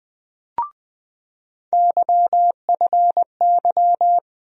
Morse code, heard as a keyed tone, about 20 wpm, 700 Hz. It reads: YFY
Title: USA Call Sign Postfixes - 3 Letters 20wpm